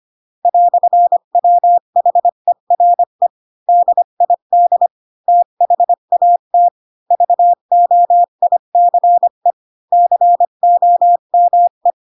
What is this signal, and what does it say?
Morse code, heard as a keyed tone, about 25 wpm, 700 Hz. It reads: "WHERE DID THAT VOICE COME
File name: Pinocchio / Pinocchio ch1 20 25 ik1hge